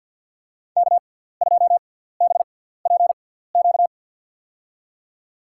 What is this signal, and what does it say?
Morse code, heard as a keyed tone, about 52 wpm, spaced as for 17 wpm, 700 Hz. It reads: K2BPQ
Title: USA Call Signs - 5 Characters - 6x Character Spacing - 50wpm